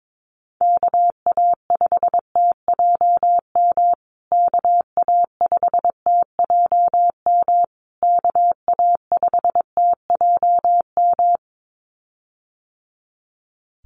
Morse code, heard as a keyed tone, about 22 wpm, 700 Hz. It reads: KA5TJM KA5TJM KA5TJM
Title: USA Call Signs Repeated 3x 22wpm